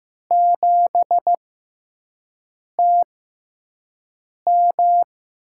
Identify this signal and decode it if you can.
Morse code, heard as a keyed tone, about 15 wpm, spaced as for 5 wpm, 700 Hz. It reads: 7TM
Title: USA Call Signs - 4 Characters - 6x Character Spacing - 15wpm